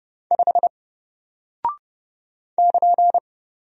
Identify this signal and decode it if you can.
Morse code, heard as a keyed tone, about 30 wpm, 700 Hz. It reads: 5 <KN>
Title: Single Character (Letter:Number:Punctuation:Prosign) Alt 30wpm